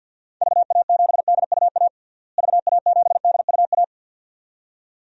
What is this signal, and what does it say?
Morse code, heard as a keyed tone, about 47 wpm, 700 Hz. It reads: VA7DUA VA7DUA
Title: DXCC-100 Call Signs - 6 Characters - Repeated 2x - 50wpm